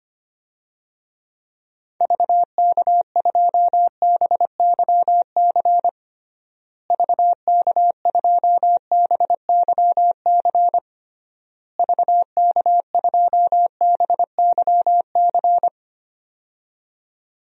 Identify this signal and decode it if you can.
Morse code, heard as a keyed tone, about 25 wpm, 700 Hz. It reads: VK2BYC VK2BYC VK2BYC
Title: DXCC-100 Call Signs - 6 Characters - Repeated 3x - 25wpm